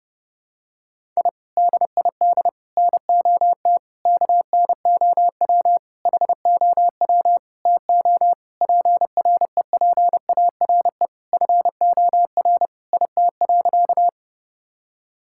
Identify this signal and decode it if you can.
Morse code, heard as a keyed tone, about 30 wpm, 700 Hz. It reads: I DID NOT KNOW HOW TO PREPARE FOR IT.
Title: Sentences from Top 1000 Words - Review Part 6 - 30wpm